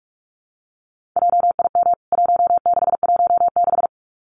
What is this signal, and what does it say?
Morse code, heard as a keyed tone, about 44 wpm, 700 Hz. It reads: JIM 1616
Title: CWT Exchange 45wpm